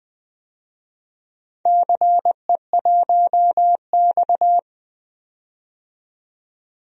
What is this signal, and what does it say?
Morse code, heard as a keyed tone, about 20 wpm, 700 Hz. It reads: CE1X
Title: DXCC-100 Call Signs - 4 Characters - 20wpm